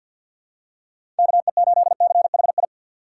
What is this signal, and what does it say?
Morse code, heard as a keyed tone, about 51 wpm, 700 Hz. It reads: KE8CHI
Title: USA Call Signs - 6 Characters - 4x Character Spacing - 50wpm